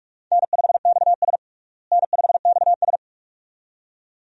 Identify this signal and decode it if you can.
Morse code, heard as a keyed tone, about 46 wpm, 700 Hz. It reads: N5XS N5XS
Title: USA Call Signs - 4 Characters - Repeated 2x - 45wpm